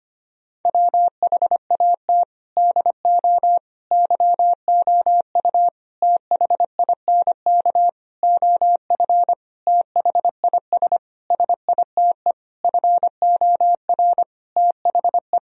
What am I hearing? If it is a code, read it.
Morse code, heard as a keyed tone, about 25 wpm, 700 Hz. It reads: WHAT DO YOU THINK OF THIS SITE FOR THE
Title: Sentences from Top 1000 Words - Review Part 1 - 25wpm